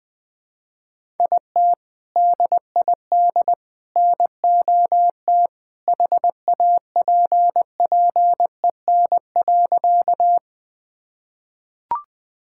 Morse code, heard as a keyed tone, about 20 wpm, 700 Hz. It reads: IT DID NOT HAPPEN.
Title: Sentences with 3-Word Phrases - 20wpm